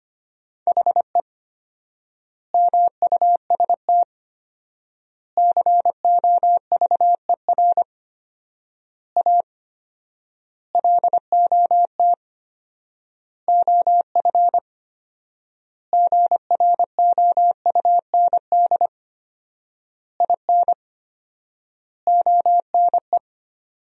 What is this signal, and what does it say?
Morse code, heard as a keyed tone, about 25 wpm, 700 Hz. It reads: HE MUST COVER A LOT OF GROUND IN ONE
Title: 4x Word-Spacing - Sentences from Top 500 Words - 25wpm